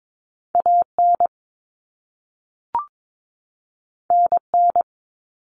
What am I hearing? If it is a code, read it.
Morse code, heard as a keyed tone, about 22 wpm, 700 Hz. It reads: AN NN
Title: Common Cut Numbers - 2-Digit 22wpm